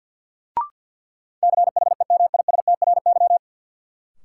Morse code, heard as a keyed tone, about 47 wpm, 700 Hz. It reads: CHEMISTRY
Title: Sentences from Top 1000 Words - Review Part 6 - New Word Drill - 50wpm